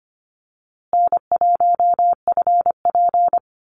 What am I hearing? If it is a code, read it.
Morse code, heard as a keyed tone, about 25 wpm, 700 Hz. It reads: N1FP
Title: USA Call Signs Repeated 3x 25wpm